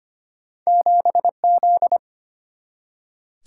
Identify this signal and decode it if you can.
Morse code, heard as a keyed tone, about 25 wpm, 700 Hz. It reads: 7Z
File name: Two Letter-Number Combinations - 25wpm